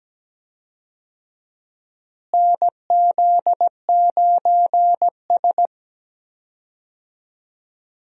Morse code, heard as a keyed tone, about 17 wpm, 700 Hz. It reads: NZ9S
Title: USA Call Signs - 4 Characters - Repeated 3x - 17wpm